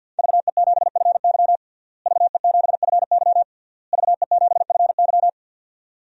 Morse code, heard as a keyed tone, about 52 wpm, 700 Hz. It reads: VE7FY VE7FY VE7FY
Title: DXCC-100 Call Signs - 5 Characters - Repeated 3x - 50wpm